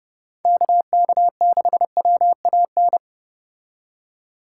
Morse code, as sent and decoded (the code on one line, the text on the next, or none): -.- -.- -.... .-- .- -.
KK6WAN